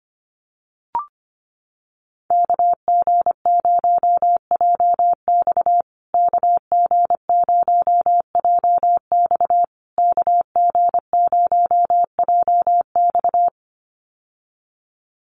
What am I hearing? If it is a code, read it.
Morse code, heard as a keyed tone, about 25 wpm, 700 Hz. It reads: KG0JX KG0JX KG0JX
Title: USA Call Signs Repeated 3x 25wpm